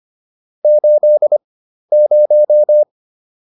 Morse code, heard as a keyed tone, about 25 wpm, spaced as for 12 wpm, 600 Hz. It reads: 80